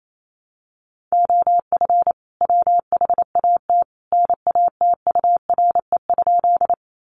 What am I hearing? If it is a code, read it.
Morse code, heard as a keyed tone, about 28 wpm, 700 Hz. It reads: OF WHAT NATURE?
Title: Sentences from Top 600 Words 28wpm